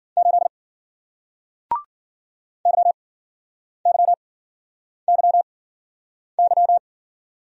Morse code, heard as a keyed tone, about 53 wpm, spaced as for 9 wpm, 700 Hz. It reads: 8 YYYY